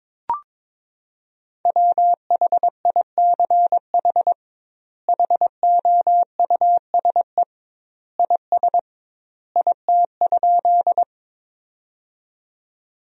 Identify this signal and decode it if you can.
Morse code, heard as a keyed tone, about 22 wpm, 700 Hz. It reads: WHICH HOUSE IS IT?
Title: 2x Word-Spacing - Sentences from Top 200 Words - 22wpm